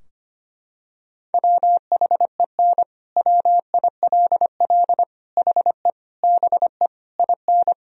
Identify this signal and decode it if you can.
Morse code, heard as a keyed tone, about 25 wpm, 700 Hz. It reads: WHEN WILL HE BE IN
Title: Sentences from Top 1000 Words - Review Part 4 - 25wpm